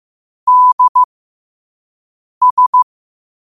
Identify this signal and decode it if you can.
Morse code, heard as a keyed tone, about 15 wpm, spaced as for 5 wpm, 1000 Hz. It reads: DS
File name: qotdfiles / 5wpm